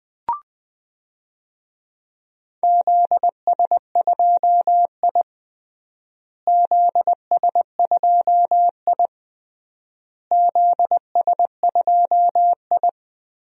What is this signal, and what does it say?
Morse code, heard as a keyed tone, about 20 wpm, 700 Hz. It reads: ZS2I ZS2I ZS2I
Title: DXCC-100 Call Signs - 4 Characters - Repeated 3x - 20wpm